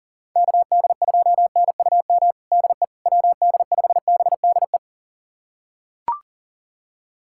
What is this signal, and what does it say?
Morse code, heard as a keyed tone, about 40 wpm, 700 Hz. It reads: KD2NUM DE WD5BDE
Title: USA Callsign DE Callsign 40wpm